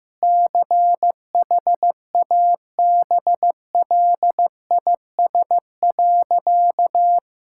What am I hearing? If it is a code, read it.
Morse code, heard as a keyed tone, about 15 wpm, 700 Hz. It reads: CHABLIS.
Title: Pangram Challenge v2 - 15wpm